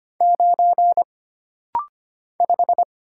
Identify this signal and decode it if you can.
Morse code, heard as a keyed tone, about 25 wpm, 700 Hz. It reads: 9 5
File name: Single Character (Letter:Number:Punctuation:Prosign) - Rapid-Fire - 25wpm